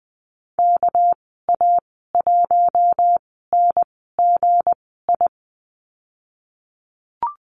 Morse code, heard as a keyed tone, about 20 wpm, spaced as for 15 wpm, 700 Hz. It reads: KA1NGI
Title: 207 Lesson 33g - Letter j - Call Signs 20wpm